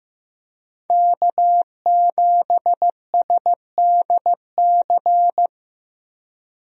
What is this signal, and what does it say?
Morse code, heard as a keyed tone, about 15 wpm, 700 Hz. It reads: K7SDC
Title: USA Call Signs - 5 Characters - Repeated 3x - 15wpm